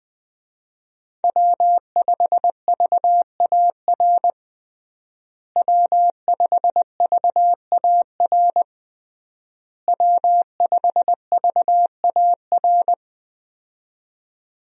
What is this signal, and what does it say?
Morse code, heard as a keyed tone, about 21 wpm, 700 Hz. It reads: W5VAR W5VAR W5VAR
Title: USA Call Signs - 5 Characters - Repeated 3x - 20wpm